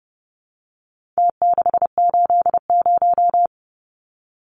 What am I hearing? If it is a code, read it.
Morse code, heard as a keyed tone, about 30 wpm, 700 Hz. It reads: T680
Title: Numbers - Contest Style - 4-Digit 30wpm